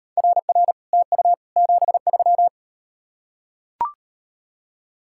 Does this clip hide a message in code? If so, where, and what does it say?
Yes: Morse code from 0.2 to 2.5 s, heard as a keyed tone, about 38 wpm, 700 Hz. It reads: RR TU 73